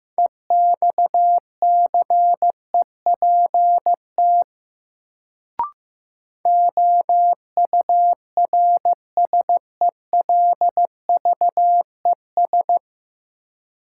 Morse code, heard as a keyed tone, about 15 wpm, 700 Hz. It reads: EXCEPT OURSELVES
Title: Sentences from Top 1600 Words - New Word Drill - 15wpm